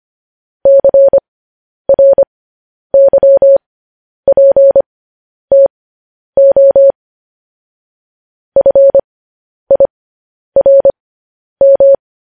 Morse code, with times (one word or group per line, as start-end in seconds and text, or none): 0.7-6.9 CRYPTO
8.6-12.0 FIRM